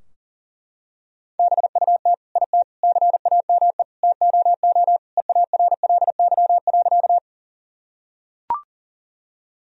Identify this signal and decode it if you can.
Morse code, heard as a keyed tone, about 40 wpm, 700 Hz. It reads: BUT IT CAME TOO EARLY.
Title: Sentences from Top 1000 Words - Review Part 2 - 40wpm